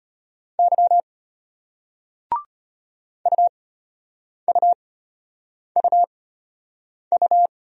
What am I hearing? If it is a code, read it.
Morse code, heard as a keyed tone, about 34 wpm, spaced as for 8 wpm, 700 Hz. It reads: Y UUUU